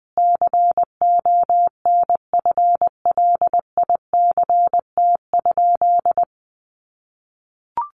Morse code, heard as a keyed tone, about 20 wpm, 700 Hz. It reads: CONFLICT?
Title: Sentences from Top 1300 Words 20wpm